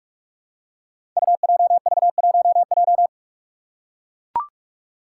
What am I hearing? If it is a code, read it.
Morse code, heard as a keyed tone, about 42 wpm, 700 Hz. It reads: UJV1J
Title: Five Challenging Letter-Number Combinations 45wpm